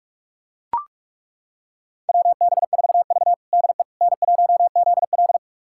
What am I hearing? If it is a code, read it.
Morse code, heard as a keyed tone, about 42 wpm, 700 Hz. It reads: WB4V DE N1ZL
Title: USA Callsign DE Callsign - 4 Characters - 45wpm